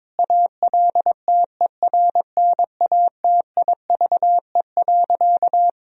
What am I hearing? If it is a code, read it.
Morse code, heard as a keyed tone, about 22 wpm, 700 Hz. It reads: ALTERNATIVE.
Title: Sentences from Top 1000 Words - Review Part 4 - 22wpm